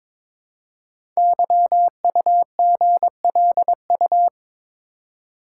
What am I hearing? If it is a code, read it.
Morse code, heard as a keyed tone, about 22 wpm, 700 Hz. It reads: YUGLU
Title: Code Groups - 5 Letter:Numbers - 22wpm